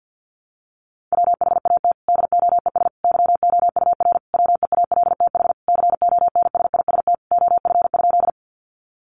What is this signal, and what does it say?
Morse code, heard as a keyed tone, about 50 wpm, 700 Hz. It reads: WHAT DOES YOUR WEALTH CONSIST OF?
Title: Sentences from Top 1400 Words 50wpm